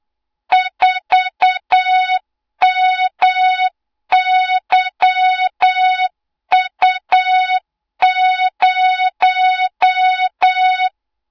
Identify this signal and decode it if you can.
Morse code, heard as a keyed tone, about 8 wpm, 3700 Hz, with noise. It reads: 4MYU0
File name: Morze-HA2MN / hang / 20 GYAKORLAT